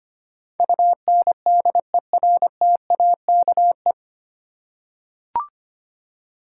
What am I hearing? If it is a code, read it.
Morse code, heard as a keyed tone, about 25 wpm, 700 Hz. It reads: UNDERTAKE
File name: Nine Letter Words - 25wpm